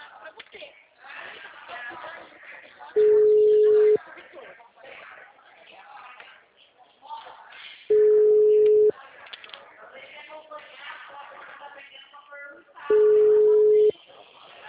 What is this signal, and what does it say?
Ringback tone (Europe, 425 Hz), with speech.